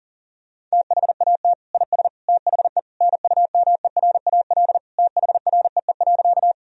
Morse code, heard as a keyed tone, about 40 wpm, 700 Hz. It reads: THAT IS THE NUMERAL THREE.
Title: Sentences from Top 1000 Words - Review Part 2 - 40wpm